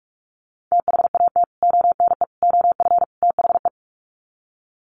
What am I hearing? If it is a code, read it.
Morse code, heard as a keyed tone, about 44 wpm, 700 Hz. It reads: THAT ONE OF THE
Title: Sets of 4 Words 45wpm